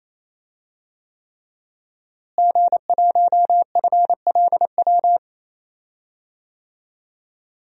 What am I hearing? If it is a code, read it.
Morse code, heard as a keyed tone, about 28 wpm, 700 Hz. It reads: G1FLW
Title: DXCC-100 Call Signs - 5 Characters - Repeated 3x - 28wpm